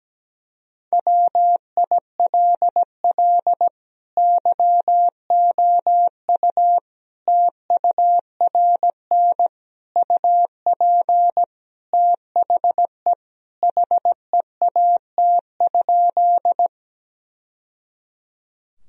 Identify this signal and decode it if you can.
Morse code, heard as a keyed tone, about 17 wpm, 700 Hz. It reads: WILL YOU TURN UP THE HEAT?
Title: Sentences from Top 1000 Words - Review Part 3 - 17wpm